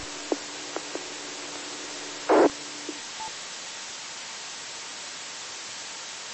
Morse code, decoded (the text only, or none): none